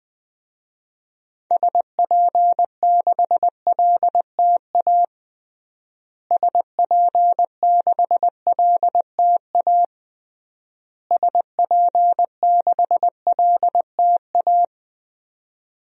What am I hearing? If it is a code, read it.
Morse code, heard as a keyed tone, about 21 wpm, 700 Hz. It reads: SP6LTA SP6LTA SP6LTA
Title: DXCC-100 Call Signs - 6 Characters - Repeated 3x - 20wpm